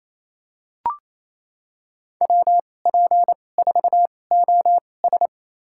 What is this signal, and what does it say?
Morse code, heard as a keyed tone, about 28 wpm, spaced as for 20 wpm, 700 Hz. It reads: WP4OS
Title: USA Call Signs - 5 Characters - 2x Character Spacing - 28wpm